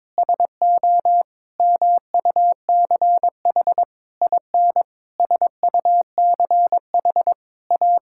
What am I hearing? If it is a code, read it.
Morse code, heard as a keyed tone, about 22 wpm, 700 Hz. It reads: SO MUCH IN SUCH A
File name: Sentences with 7-Word Phrases - 22wpm